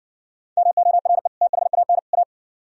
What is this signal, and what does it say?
Morse code, heard as a keyed tone, about 60 wpm, 700 Hz. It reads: MORE THAN A